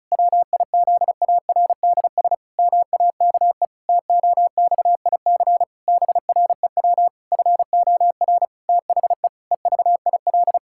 Morse code, heard as a keyed tone, about 36 wpm, 700 Hz. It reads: WIZARDS MAKE TOXIC BREW FOR THE EVIL
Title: Pangram Challenge v2 - 35wpm